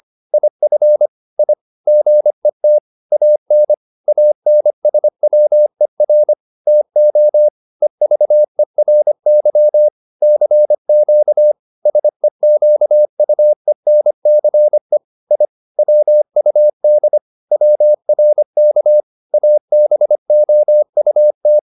Morse code, heard as a keyed tone, about 25 wpm, 600 Hz. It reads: IF I GET AN ANSWER TO EVERY CQ SEQUENCE I WUD WRK ABOUT